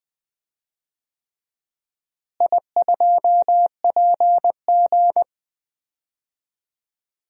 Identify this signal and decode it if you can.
Morse code, heard as a keyed tone, about 20 wpm, 700 Hz. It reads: I2PG
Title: DXCC-100 Call Signs - 4 Characters - Repeated 3x - 20wpm